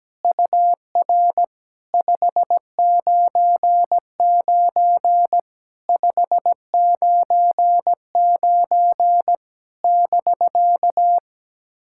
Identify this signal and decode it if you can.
Morse code, heard as a keyed tone, about 17 wpm, 690 Hz. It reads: UR 599 599 <BK>
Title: SOTA QSO Practice 17wpm